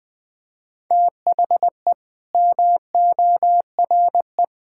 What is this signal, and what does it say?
Morse code, heard as a keyed tone, about 21 wpm, 700 Hz. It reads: THE MORE
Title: Sets of 8 Words - 20wpm